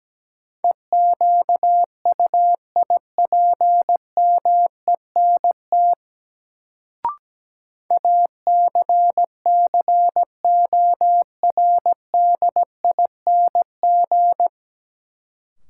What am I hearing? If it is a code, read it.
Morse code, heard as a keyed tone, about 17 wpm, 700 Hz. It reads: EQUIPMENT ACCORDING
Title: Sentences from Top 700 Words - New Word Drill - 17wpm